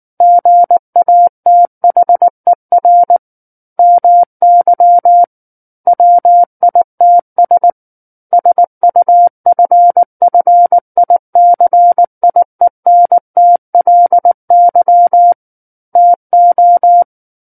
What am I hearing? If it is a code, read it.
Morse code, heard as a keyed tone, about 19 wpm, 700 Hz. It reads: GATHER MY WITS SUFFICIENTLY TO
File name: Book One / 10-Chapter 10 - 19 WPM